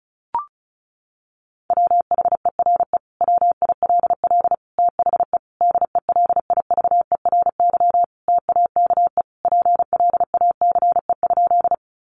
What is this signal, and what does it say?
Morse code, heard as a keyed tone, about 34 wpm, 700 Hz. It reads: WHERE WILL THE DELIVERY TAKE PLACE?
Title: Sentences from Top 1500 Words 35wpm